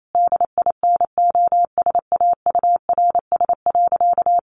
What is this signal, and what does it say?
Morse code, heard as a keyed tone, about 28 wpm, 700 Hz. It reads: DINOSAURS.
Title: A Sound Of Thunder 28wpm